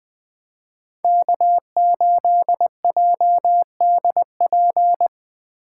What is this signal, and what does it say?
Morse code, heard as a keyed tone, about 22 wpm, 700 Hz. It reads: K8JDP
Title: Speed Racing - USA Call Signs 20wpm